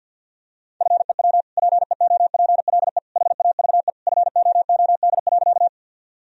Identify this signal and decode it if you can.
Morse code, heard as a keyed tone, about 51 wpm, 700 Hz. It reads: FEW PEOPLE HAVE FOOD.
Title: Sentences from Top 1000 Words - Review Part 1 - 50wpm